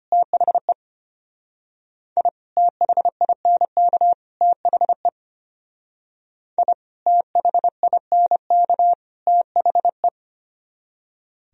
Morse code, heard as a keyed tone, about 30 wpm, 700 Hz. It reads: THE I THINK THE I THINK THE
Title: Speed-Racing - Sets of 3 Words 25wpm